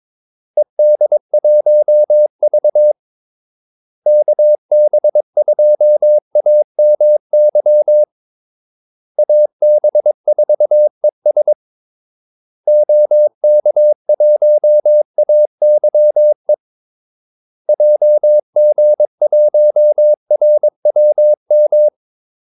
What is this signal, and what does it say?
Morse code, heard as a keyed tone, about 22 wpm, 600 Hz. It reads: ED1V KB2AMY AB4ES OK1AYE JG1RWM
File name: Callsigns-22wpm